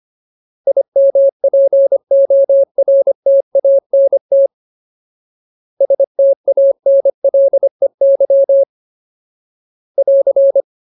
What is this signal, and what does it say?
Morse code, heard as a keyed tone, about 25 wpm, 550 Hz. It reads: IMPORTANT STANLEY <AR>
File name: Ultimate Word List / 25WPM Ultimate